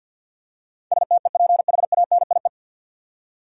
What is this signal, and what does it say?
Morse code, heard as a keyed tone, about 48 wpm, 700 Hz. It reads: STEPHANIE
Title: USA Common Names - 50wpm